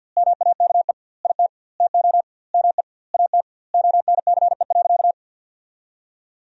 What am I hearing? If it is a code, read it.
Morse code, heard as a keyed tone, about 45 wpm, 700 Hz. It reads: MAKE IT TO ME AT ONCE.
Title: 2x Word-Spacing - Sentences from Top 500 Words - 50wpm